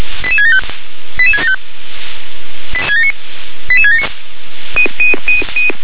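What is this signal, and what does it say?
Five-tone selective call (ZVEI), heard as quick runs of five tones, tones of about 70 ms. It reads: 88675 88675 88679
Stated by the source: Five-tone ZVEI selective call 88774 received by SDR.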